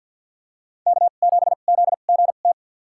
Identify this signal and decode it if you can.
Morse code, heard as a keyed tone, about 49 wpm, spaced as for 36 wpm, 700 Hz. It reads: K7ZGT
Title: USA Call Signs - 5 Characters - 2x Character Spacing - 50wpm